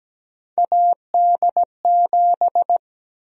Morse code, heard as a keyed tone, about 17 wpm, 700 Hz. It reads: AD7
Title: USA Call Sign Prefixes - Repeated 2x - 17wpm